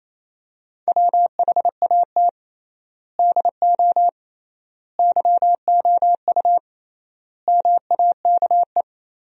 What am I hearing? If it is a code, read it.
Morse code, heard as a keyed tone, about 28 wpm, 700 Hz. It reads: WHAT DO YOU MAKE
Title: 3x Word-Spacing - Sentences from Top 100 Words - 28wpm